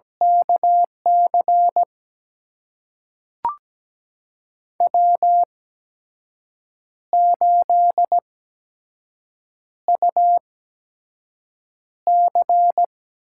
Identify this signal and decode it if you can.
Morse code, heard as a keyed tone, about 17 wpm, spaced as for 6 wpm, 700 Hz. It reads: KC W8UC
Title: USA Call Signs - 5 Characters - 8x Character Spacing - 17wpm